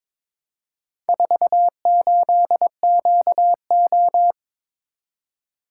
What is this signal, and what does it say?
Morse code, heard as a keyed tone, about 22 wpm, 700 Hz. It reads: 48QO
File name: practice / 22wpm / Four Letter-Number Combinations - 22wpm